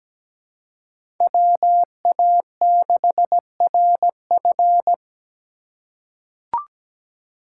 Morse code, heard as a keyed tone, about 17 wpm, 700 Hz. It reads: WA6RF
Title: USA Call Signs - 5 Characters - 4x Character Spacing - 17wpm